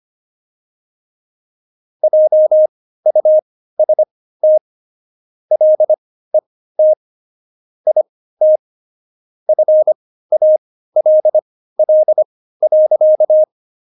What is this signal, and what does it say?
Morse code, heard as a keyed tone, about 25 wpm, spaced as for 15 wpm, 620 Hz. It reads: JUST LET IT FALL.
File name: Jokes-25-char-15-eff